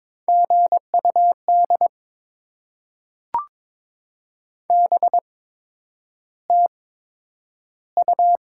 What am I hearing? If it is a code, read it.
Morse code, heard as a keyed tone, about 22 wpm, spaced as for 9 wpm, 700 Hz. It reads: GUD BTU
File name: Top 20 QSO Elements - 8x Character Spacing - 22wpm